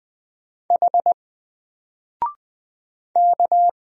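Morse code, heard as a keyed tone, about 20 wpm, 700 Hz. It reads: H K
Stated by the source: Single letter and number Morse drill at 20 wpm, from N4LCD.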